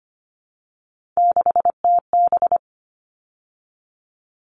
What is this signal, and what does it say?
Morse code, heard as a keyed tone, about 25 wpm, 700 Hz. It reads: 6TB
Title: All Cut Numbers - 3-Digit 25wpm